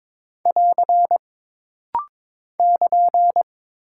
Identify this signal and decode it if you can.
Morse code, heard as a keyed tone, about 22 wpm, 700 Hz. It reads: <AR> <KN>
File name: practice / 22wpm / Single Punctuation-Prosign - Rapid-Fire - 22wpm